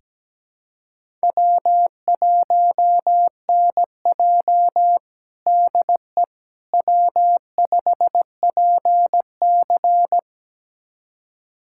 Morse code, heard as a keyed tone, about 17 wpm, 700 Hz. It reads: W1NJ DE W5PC
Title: USA Callsign DE Callsign - 4 Characters - 17wpm